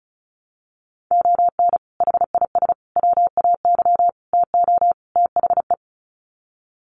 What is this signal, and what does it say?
Morse code, heard as a keyed tone, about 34 wpm, 700 Hz. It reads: ON HIS WAY TO THE
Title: Sets of 5 Words Encore2 35wpm